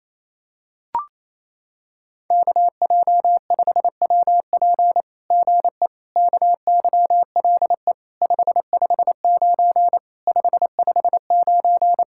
Morse code, heard as a keyed tone, about 28 wpm, 700 Hz. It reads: KJ5WP GE KYLE 559 559
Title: SOTA QSO Practice 28wpm